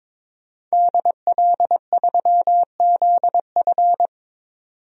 Morse code, heard as a keyed tone, about 22 wpm, 700 Hz. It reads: DL3ZF
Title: DXCC-100 Call Signs - 5 Characters - 22wpm